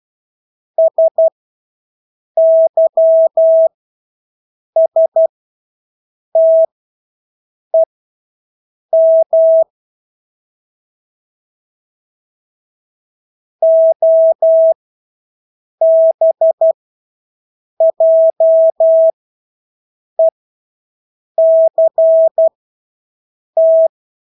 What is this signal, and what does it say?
Morse code, heard as a keyed tone, about 12 wpm, spaced as for 6 wpm, 650 Hz. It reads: SYSTEM OBJECT
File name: Engelske ord 12-6 wpm